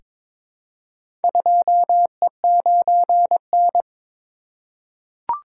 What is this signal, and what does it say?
Morse code, heard as a keyed tone, about 22 wpm, 700 Hz. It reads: 2E9N
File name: practice / 22wpm / Four Letter-Number Combinations - 22wpm